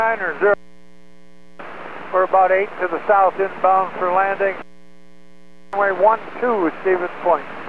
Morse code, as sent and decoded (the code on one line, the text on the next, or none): none